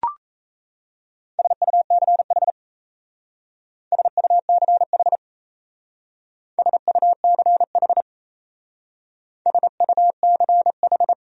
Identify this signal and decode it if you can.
Morse code, heard as a keyed tone, about 34 wpm, 700 Hz. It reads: SUCH SUCH SUCH SUCH
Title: Speed-Racing Top 300 Words 28wpm